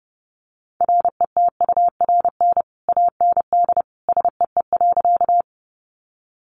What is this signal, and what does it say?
Morse code, heard as a keyed tone, about 30 wpm, 700 Hz. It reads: RETURN AND SEE.
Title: Sentences from Top 800 Words 30wpm